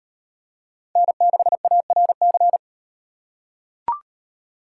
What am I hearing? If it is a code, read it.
Morse code, heard as a keyed tone, about 37 wpm, 700 Hz. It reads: N6ARC